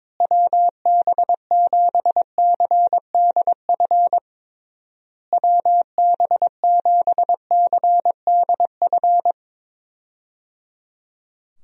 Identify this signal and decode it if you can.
Morse code, heard as a keyed tone, about 22 wpm, 700 Hz. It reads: WB7CDF WB7CDF
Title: USA Call Signs - 6 Characters - Repeated 3x - 22wpm